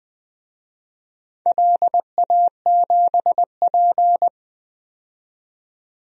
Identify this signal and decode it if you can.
Morse code, heard as a keyed tone, about 20 wpm, 700 Hz. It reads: LA7P
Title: DXCC-100 Call Signs - 4 Characters - 20wpm